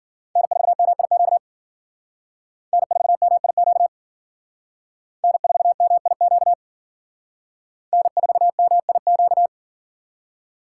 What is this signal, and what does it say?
Morse code, heard as a keyed tone, about 47 wpm, 700 Hz. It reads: N4MIQ N4MIQ N4MIQ N4MIQ
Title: Speed Racing - USA 1x3 Call Signs 40wpm